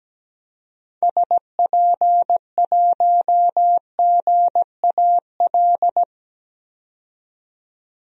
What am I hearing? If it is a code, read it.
Morse code, heard as a keyed tone, about 17 wpm, 700 Hz. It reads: SP1GAL
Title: DXCC-100 Call Signs - 6 Characters - 17wpm